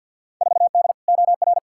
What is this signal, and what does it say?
Morse code, heard as a keyed tone, about 52 wpm, 700 Hz. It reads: 4D OR
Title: Field Day Exchanges Encore - 50wpm